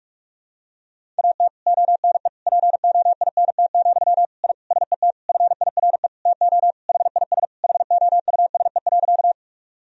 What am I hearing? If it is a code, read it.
Morse code, heard as a keyed tone, about 46 wpm, 700 Hz. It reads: AT ONE POINT, I SET FIRE TO HIS HOUSE.
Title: Sentences from Top 1000 Words - Review Part 1 - 45wpm